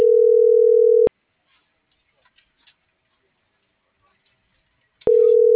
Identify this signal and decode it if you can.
Ringback tone (North America), 440 + 480 Hz.